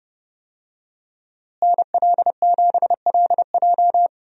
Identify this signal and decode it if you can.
Morse code, heard as a keyed tone, about 30 wpm, 700 Hz. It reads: NL7LJ
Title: USA Call Signs - 5 Characters - Repeated 2x - 30wpm